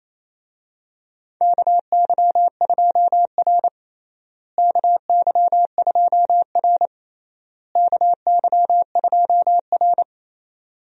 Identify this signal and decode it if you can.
Morse code, heard as a keyed tone, about 28 wpm, 700 Hz. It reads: KY2R KY2R KY2R
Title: USA Call Signs - 4 Characters - Repeated 3x - 28wpm